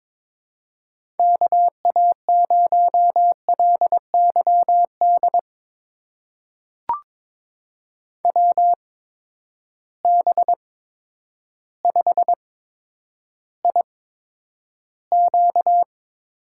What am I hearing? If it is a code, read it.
Morse code, heard as a keyed tone, about 22 wpm, spaced as for 10 wpm, 700 Hz. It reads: KA0LYD WB5IQ
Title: USA Call Signs - 6 Characters - 8x Character Spacing - 22wpm